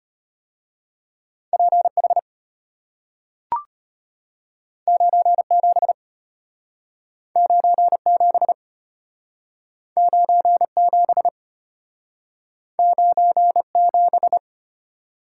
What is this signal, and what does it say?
Morse code, heard as a keyed tone, about 32 wpm, 700 Hz. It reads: PH 97 97 97 97